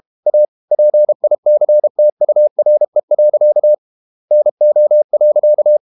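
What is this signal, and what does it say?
Morse code, heard as a keyed tone, about 32 wpm, 600 Hz. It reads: A PICTURE. NO.
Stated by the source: Morse code practice at 32 wpm from CW med Gnister, reading Ken Follett's novel Never in English.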